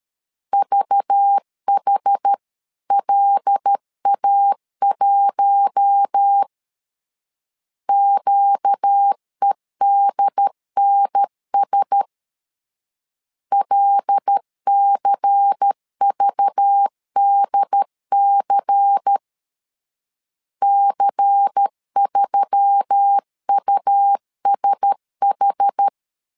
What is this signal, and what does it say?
Morse code, heard as a keyed tone, about 13 wpm, 800 Hz. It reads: VHLA1 QEDNS LCVDC C3USH
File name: LEZIONI / 5 LBVC / 5 LBVC 12wpm 1955